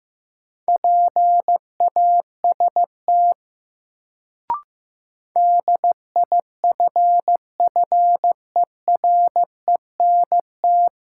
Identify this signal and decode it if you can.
Morse code, heard as a keyed tone, about 15 wpm, 700 Hz. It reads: PAST DIFFERENT
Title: Sentences from Top 600 Words - New Word Drill - 15wpm